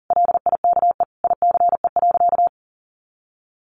Morse code, heard as a keyed tone, about 40 wpm, 700 Hz. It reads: LIKE ICE.
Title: Sentences from Top 1000 Words 40wpm